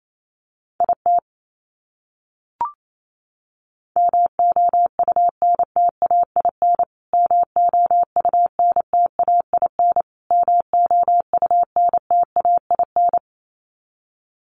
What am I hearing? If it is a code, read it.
Morse code, heard as a keyed tone, about 28 wpm, 700 Hz. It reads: IT MOUNTAIN MOUNTAIN MOUNTAIN
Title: Top 500 Words Repeated 28wpm